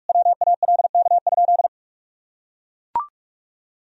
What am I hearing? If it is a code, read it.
Morse code, heard as a keyed tone, about 42 wpm, 700 Hz. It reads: WALK?
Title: Sentences from Top 1000 Words - Review Part 1 - 45wpm